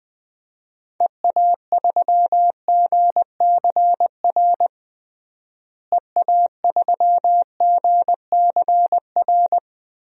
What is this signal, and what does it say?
Morse code, heard as a keyed tone, about 20 wpm, 700 Hz. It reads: EA3GCR EA3GCR
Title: DXCC-100 Call Signs - 6 Characters - Repeated 3x - 20wpm